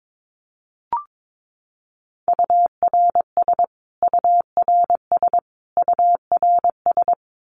Morse code, heard as a keyed tone, about 22 wpm, 700 Hz. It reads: URS URS URS
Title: Abbreviations Repeated 3x - 22wpm